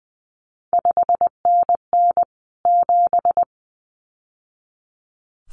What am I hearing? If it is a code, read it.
Morse code, heard as a keyed tone, about 20 wpm, 700 Hz. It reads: 5NN 7
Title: CQ WW DX Contest Exchange - 20wpm